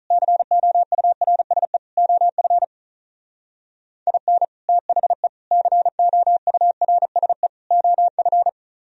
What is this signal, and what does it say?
Morse code, heard as a keyed tone, about 44 wpm, 700 Hz. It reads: COURSE OF IN THE COURSE OF